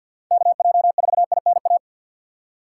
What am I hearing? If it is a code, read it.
Morse code, heard as a keyed tone, about 48 wpm, 700 Hz. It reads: KJ4INA
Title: Speed Racing - USA 2x3 Call Signs 50wpm